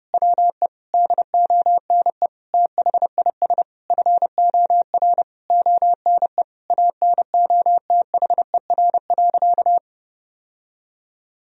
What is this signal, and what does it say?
Morse code, heard as a keyed tone, about 32 wpm, 700 Hz. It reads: WE DONE THIS FOR ONE ANOTHER.
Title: Sentences from Top 1000 Words - Review Part 1 - 30wpm